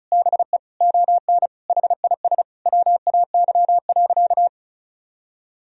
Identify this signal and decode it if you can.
Morse code, heard as a keyed tone, about 36 wpm, 700 Hz. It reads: BE ON HIS WAY.
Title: Sentences from Top 300 Words and Abbreviations 35wpm